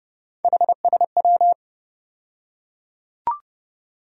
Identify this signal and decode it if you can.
Morse code, heard as a keyed tone, about 30 wpm, 700 Hz. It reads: HSW
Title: USA Call Sign Postfixes - 3 Letters 30wpm